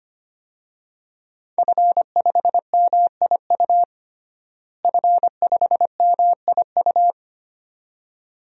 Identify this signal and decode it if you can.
Morse code, heard as a keyed tone, about 25 wpm, 700 Hz. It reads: F5MIU F5MIU
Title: DXCC-100 Call Signs - 5 Characters - Repeated 2x - 25wpm